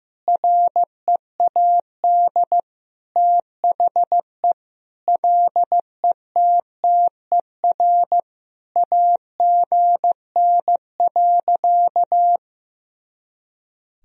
Morse code, heard as a keyed tone, about 15 wpm, 700 Hz. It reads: READ THE LETTER AGN.
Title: Sentences from Top 300 Words and Abbreviations 15wpm